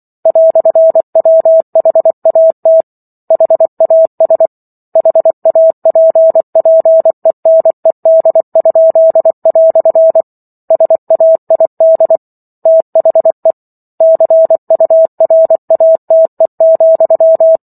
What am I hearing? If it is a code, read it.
Morse code, heard as a keyed tone, about 24 wpm, 650 Hz. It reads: "WHAT HAS HAPPENED?" SAID THE CURATE,